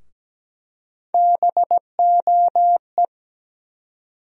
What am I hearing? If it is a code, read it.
Morse code, heard as a keyed tone, about 17 wpm, 700 Hz. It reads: BOE
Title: USA Call Sign Postfixes - 3 Letters 17wpm